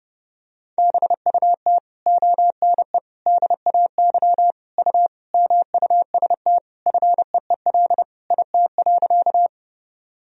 Morse code, heard as a keyed tone, about 30 wpm, 700 Hz. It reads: BUT ONE DAY U MUST FEEL IT.
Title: Sentences from Top 400 Words and Abbreviations 30wpm